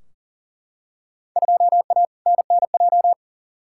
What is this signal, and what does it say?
Morse code, heard as a keyed tone, about 37 wpm, 700 Hz. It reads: 2A NNJ
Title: Field Day Exchanges - 40wpm